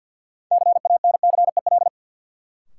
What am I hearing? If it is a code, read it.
Morse code, heard as a keyed tone, about 52 wpm, 700 Hz. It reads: CANCEL
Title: Sentences from Top 1200 Words - New Word Drill Alt - 50wpm